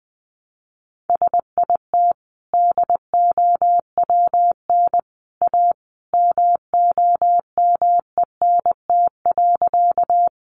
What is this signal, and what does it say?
Morse code, heard as a keyed tone, about 20 wpm, 700 Hz. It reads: SIT DOWN A MOMENT.
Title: Sentences from Top 1000 Words 20wpm